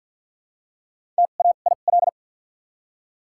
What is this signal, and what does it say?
Morse code, heard as a keyed tone, about 54 wpm, spaced as for 37 wpm, 700 Hz. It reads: TAIL
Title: 2x Character Spacing - Top 500 Words - 50wpm